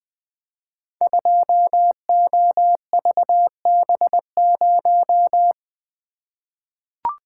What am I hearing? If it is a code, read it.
Morse code, heard as a keyed tone, about 20 wpm, 700 Hz. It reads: 2OVB0
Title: Code Groups - 5 Letter:Numbers - 20wpm